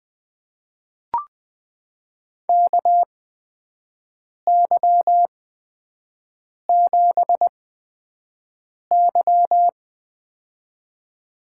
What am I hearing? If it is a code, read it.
Morse code, heard as a keyed tone, about 20 wpm, spaced as for 5 wpm, 700 Hz. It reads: KY7Y
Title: USA Call Signs - 4 Characters - 8x Character Spacing - 20wpm